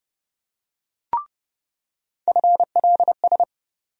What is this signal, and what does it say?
Morse code, heard as a keyed tone, about 30 wpm, 700 Hz. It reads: FLS
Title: USA Call Sign Postfixes - 3 Letters 30wpm